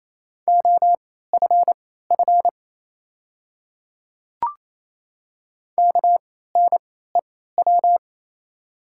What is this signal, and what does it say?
Morse code, heard as a keyed tone, about 28 wpm, spaced as for 16 wpm, 700 Hz. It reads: OFF KNEW
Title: 3x Character Spacing - Top 500 Words - 28wpm